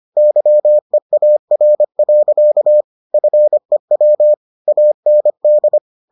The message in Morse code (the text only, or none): YEAR. FEW AND